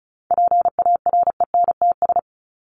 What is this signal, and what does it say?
Morse code, heard as a keyed tone, about 34 wpm, 700 Hz. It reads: PARENTS
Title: ES:S Suffix - Top 100 Words with Base Words 35wpm